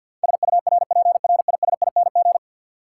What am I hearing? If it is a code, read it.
Morse code, heard as a keyed tone, about 53 wpm, 700 Hz. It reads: SURPRISING
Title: Sets of 5 Words - New Word Drill - 50wpm